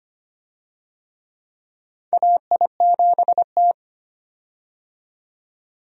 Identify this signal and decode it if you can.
Morse code, heard as a keyed tone, about 25 wpm, 700 Hz. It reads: AI7T
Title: USA Call Signs - 4 Characters - 25wpm